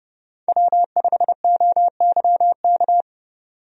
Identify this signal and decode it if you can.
Morse code, heard as a keyed tone, about 30 wpm, 700 Hz. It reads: W5OYK
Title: USA Call Signs - 5 Characters - 30wpm